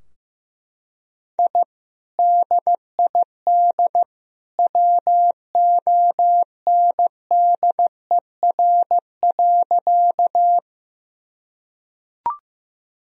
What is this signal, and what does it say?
Morse code, heard as a keyed tone, about 15 wpm, 700 Hz. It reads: I DID WONDER.
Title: Sentences from Top 1000 Words - Review Part 3 - 15wpm